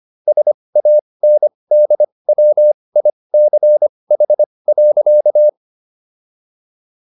Morse code, heard as a keyed tone, about 25 wpm, spaced as for 20 wpm, 600 Hz. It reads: SANDWICH.